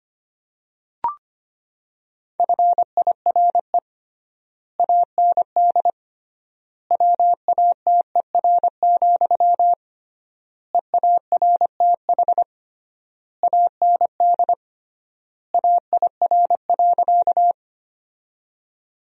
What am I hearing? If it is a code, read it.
Morse code, heard as a keyed tone, about 25 wpm, 700 Hz. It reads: FIRE AND WATER, EARTH AND AIR.
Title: 3x Word-Spacing - Sentences from Top 400 Words - 25wpm